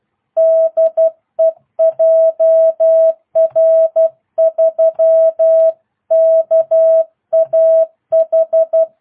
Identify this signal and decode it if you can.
Morse code, heard as a keyed tone, about 12 wpm, 650 Hz. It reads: DE JR3KAH